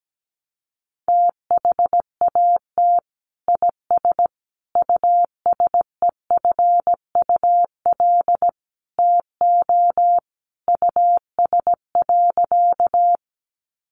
Morse code, heard as a keyed tone, about 17 wpm, 700 Hz. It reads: THAT IS USEFUL TO US.
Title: Sentences from Top 700 Words 17wpm